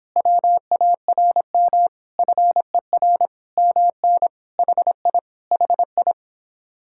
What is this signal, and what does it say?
Morse code, heard as a keyed tone, about 26 wpm, 700 Hz. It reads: WARM FER MN HI HI